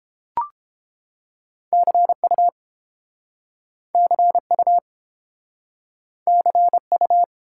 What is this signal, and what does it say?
Morse code, heard as a keyed tone, about 30 wpm, 700 Hz. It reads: CU CU CU